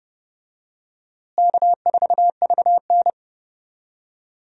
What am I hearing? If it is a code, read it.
Morse code, heard as a keyed tone, about 30 wpm, 700 Hz. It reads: K4VN
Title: USA Call Signs - 4 Characters - Repeated 2x - 30wpm